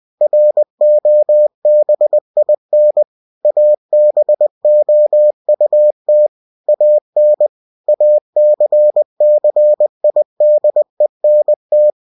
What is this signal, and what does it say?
Morse code, heard as a keyed tone, about 20 wpm, 600 Hz. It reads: ROBIN ABOUT AN ACCIDENT
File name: cw-books / winnie-the-pooh / winnie-20wpm-page-0106